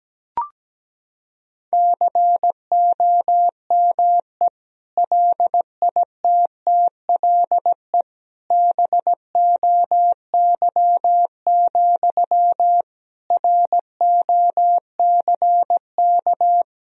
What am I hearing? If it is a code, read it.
Morse code, heard as a keyed tone, about 17 wpm, 700 Hz. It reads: COME LITTLE BOY, ROCK
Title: Sentences from Top 1000 Words - Review Part 2 - 17wpm